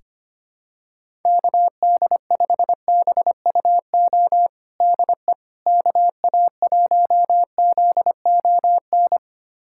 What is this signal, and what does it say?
Morse code, heard as a keyed tone, about 25 wpm, 700 Hz. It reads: KD5BUO DE KA1ZON
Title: USA Callsign DE Callsign - 6 Characters - 25wpm